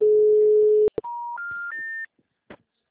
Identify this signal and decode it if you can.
Special information tone.